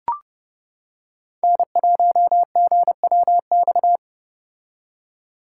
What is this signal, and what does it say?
Morse code, heard as a keyed tone, about 29 wpm, 700 Hz. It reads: N1GWX